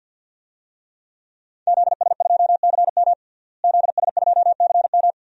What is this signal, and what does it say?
Morse code, heard as a keyed tone, about 51 wpm, 700 Hz. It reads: ZS2CM ZS2CM
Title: DXCC-100 Call Signs - 5 Characters - Repeated 2x - 50wpm